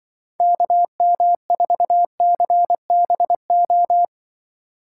Morse code, heard as a keyed tone, about 23 wpm, 700 Hz. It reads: KM4CBO
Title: Speed Racing - USA 2x3 Call Signs 20wpm